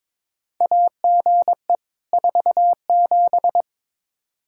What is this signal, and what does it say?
Morse code, heard as a keyed tone, about 22 wpm, 700 Hz. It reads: AGE 47
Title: K1USN SST QSO Practice 22wpm